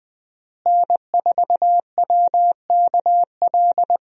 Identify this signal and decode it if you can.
Morse code, heard as a keyed tone, about 20 wpm, 700 Hz. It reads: N4WKL
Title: USA Call Signs - 5 Characters - Repeated 3x - 20wpm